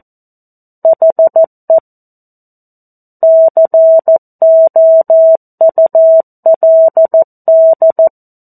Morse code, heard as a keyed tone, about 14 wpm, 650 Hz. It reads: HE COULD
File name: 02-Chapter-02 14 WPM